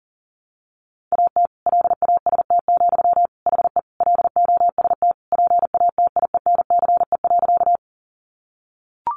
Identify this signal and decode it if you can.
Morse code, heard as a keyed tone, about 40 wpm, 700 Hz. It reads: AT LAST, HE LOST PATIENCE.
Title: Sentences from Top 1400 Words 40wpm